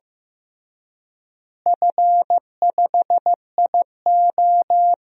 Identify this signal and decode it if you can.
Morse code, heard as a keyed tone, about 15 wpm, 700 Hz. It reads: F5IO
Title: DXCC-100 Call Signs - 4 Characters - Repeated 3x - 15wpm